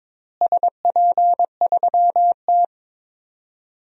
Morse code, heard as a keyed tone, about 21 wpm, 700 Hz. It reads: SP3T
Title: DXCC-100 Call Signs - 4 Characters - Repeated 3x - 22wpm